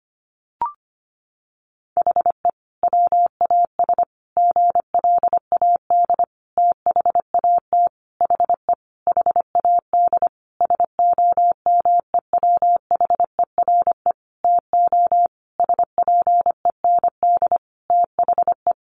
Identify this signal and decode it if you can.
Morse code, heard as a keyed tone, about 25 wpm, 700 Hz. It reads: HE WAS GLAD THAT HE HAD SOMEWHERE TO SPEND THE
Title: Sentences from Top 1400 Words 25wpm